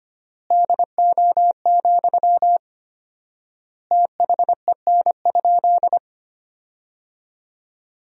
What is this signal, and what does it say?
Morse code, heard as a keyed tone, about 25 wpm, 700 Hz. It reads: DO, THEN?
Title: 4x Word-Spacing - Sentences from Top 100 Words - 25wpm